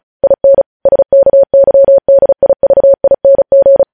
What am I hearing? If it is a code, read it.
Morse code, heard as a keyed tone, about 36 wpm, 550 Hz. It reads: IN SKYDIVING
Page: news headlines in morse code at 35 wpm